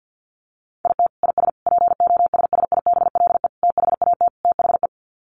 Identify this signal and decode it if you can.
Morse code, heard as a keyed tone, about 50 wpm, 700 Hz. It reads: IT IS POSSIBLE THAT THE
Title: Sets of 5 Words Encore2 50wpm